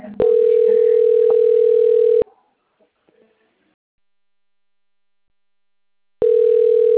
Ringback tone (North America), 440 + 480 Hz.